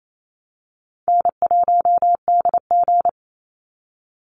Morse code, heard as a keyed tone, about 28 wpm, 700 Hz. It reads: N1DG
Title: Easy Call Signs Repeated 2x - 28wpm